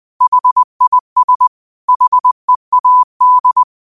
Morse code, heard as a keyed tone, about 20 wpm, 1000 Hz. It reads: HIS HEAD